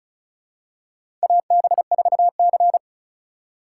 Morse code, heard as a keyed tone, about 36 wpm, 700 Hz. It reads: AB4C
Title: USA Call Signs - 4 Characters - 4x Character Spacing - 35wpm